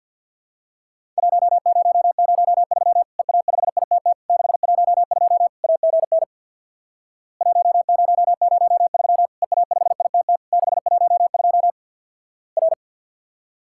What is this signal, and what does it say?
Morse code, heard as a keyed tone, about 51 wpm, 700 Hz. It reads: 1003 EA5ITT 612 AGN 1003 EA5ITT 612 R
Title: Worked All Europe DX Contest - QTC Exchange with Repeats - 50wpm